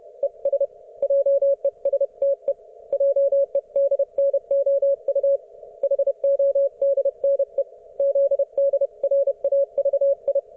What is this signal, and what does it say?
Morse code, heard as a keyed tone, about 31 wpm, 560 Hz. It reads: ES JESTE JEDNOU HODNE ZDRAVI